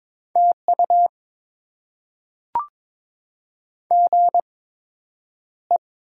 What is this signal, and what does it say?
Morse code, heard as a keyed tone, about 22 wpm, spaced as for 9 wpm, 700 Hz. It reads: TU GE
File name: Top 20 QSO Elements - 8x Character Spacing - 22wpm